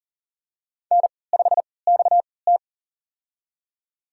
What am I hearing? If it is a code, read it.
Morse code, heard as a keyed tone, about 40 wpm, spaced as for 23 wpm, 700 Hz. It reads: N5XT
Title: USA Call Signs - 4 Characters - 3x Character Spacing - 40wpm